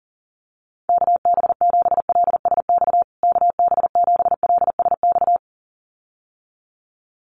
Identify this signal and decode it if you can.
Morse code, heard as a keyed tone, about 40 wpm, 700 Hz. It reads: KB7LSX KB7LSX
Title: USA Call Signs Repeated 2x 40wpm